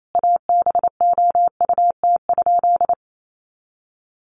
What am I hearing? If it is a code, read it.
Morse code, heard as a keyed tone, about 28 wpm, 700 Hz. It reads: ABOUT?
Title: Sentences from Top 100 Words 28wpm